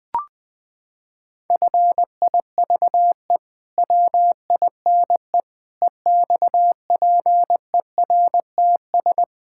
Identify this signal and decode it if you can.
Morse code, heard as a keyed tone, about 21 wpm, 700 Hz. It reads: FIVE WINE EXPERTS
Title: Pangram Challenge v2 - 20wpm